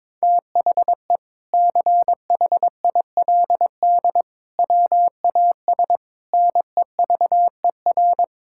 Morse code, heard as a keyed tone, about 22 wpm, 700 Hz. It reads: THE CHILD WAS NEVER
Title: Sentences from Top 1000 Words - Review Part 6 - 22wpm